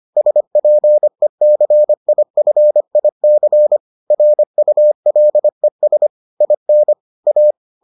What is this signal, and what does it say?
Morse code, heard as a keyed tone, about 25 wpm, 600 Hz. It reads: SPECIFIC RULES IN A